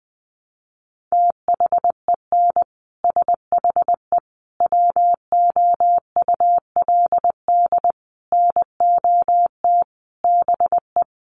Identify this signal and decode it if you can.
Morse code, heard as a keyed tone, about 20 wpm, 700 Hz. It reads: THEN SHE WOULD NOT BE
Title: Sentences from Top 1400 Words 20wpm